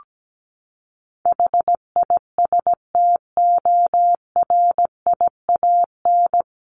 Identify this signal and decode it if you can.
Morse code, heard as a keyed tone, about 17 wpm, 700 Hz. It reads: HISTORIAN
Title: Top 1751-2000 Words 17wpm